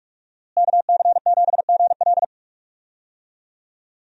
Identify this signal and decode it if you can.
Morse code, heard as a keyed tone, about 47 wpm, 700 Hz. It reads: KC7GL
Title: USA Call Signs - 5 Characters - Repeated 2x - 45wpm